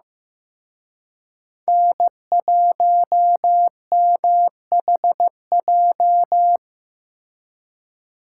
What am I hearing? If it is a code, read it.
Morse code, heard as a keyed tone, about 18 wpm, 700 Hz. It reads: N1MHJ
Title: Speed Racing - USA Call Signs 15wpm